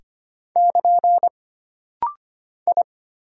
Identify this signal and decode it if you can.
Morse code, heard as a keyed tone, about 25 wpm, 700 Hz. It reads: <KN> I
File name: Single Character (Letter:Number:Punctuation:Prosign) - Rapid-Fire - 25wpm